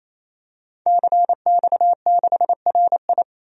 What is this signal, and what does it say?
Morse code, heard as a keyed tone, about 28 wpm, 700 Hz. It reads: CX6RI
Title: DXCC-100 Call Signs - 5 Characters - 28wpm